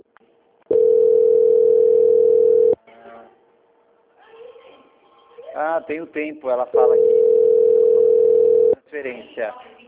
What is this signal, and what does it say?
Ringback tone (North America), 440 + 480 Hz, with speech.
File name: out- -2002- - - .7392